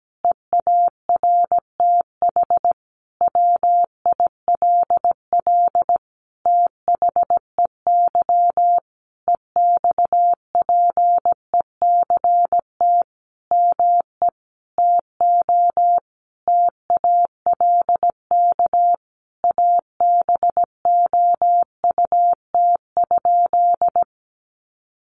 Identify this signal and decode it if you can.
Morse code, heard as a keyed tone, about 17 wpm, 700 Hz. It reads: EARTH WILL THEY EXPECT ME TO TALK ABOUT?
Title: Sentences from Top 1100 Words 17wpm